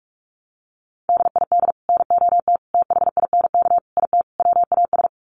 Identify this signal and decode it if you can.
Morse code, heard as a keyed tone, about 44 wpm, 700 Hz. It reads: DID NOT THINK IT WAS
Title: Sets of 5 Words Encore 45wpm